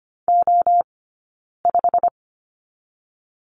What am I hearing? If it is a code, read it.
Morse code, heard as a keyed tone, about 25 wpm, 700 Hz. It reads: O 5
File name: course / 25:10wpm / 099 Lesson 17d - Letter u - Two Characters 25wpm